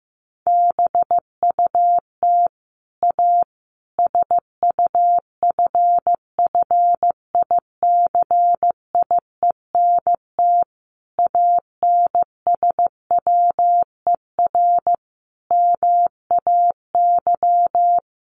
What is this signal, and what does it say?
Morse code, heard as a keyed tone, about 15 wpm, 700 Hz. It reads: BUT A SUFFICIENT ANSWER MAY
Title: Sentences from Top 1400 Words 15wpm